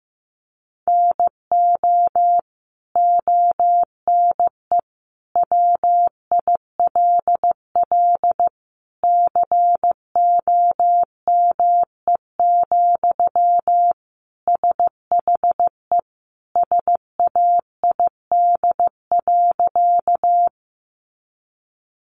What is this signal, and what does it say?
Morse code, heard as a keyed tone, about 15 wpm, 700 Hz. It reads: NO ONE WILL COME, SHE SAID.
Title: Sentences from Top 100 Words 15wpm